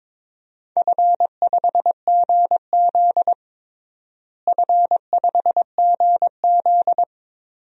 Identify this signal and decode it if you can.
Morse code, heard as a keyed tone, about 22 wpm, 700 Hz. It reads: F5GZ F5GZ
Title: DXCC-100 Call Signs - 4 Characters - Repeated 3x - 22wpm